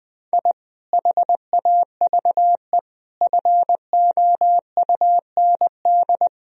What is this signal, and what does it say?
Morse code, heard as a keyed tone, about 20 wpm, 700 Hz. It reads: I HAVE FOUND
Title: Sentences from Top 1000 Words - Review Part 3 - 20wpm